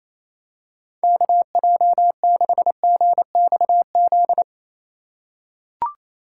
Morse code, heard as a keyed tone, about 28 wpm, 700 Hz. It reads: KJ6GXZ
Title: USA Call Signs - 6 Characters - 2x Character Spacing - 28wpm